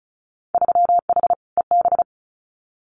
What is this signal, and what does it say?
Morse code, heard as a keyed tone, about 36 wpm, 700 Hz. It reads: 3H EB
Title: Winter Field Day Exchanges Encore 35wpm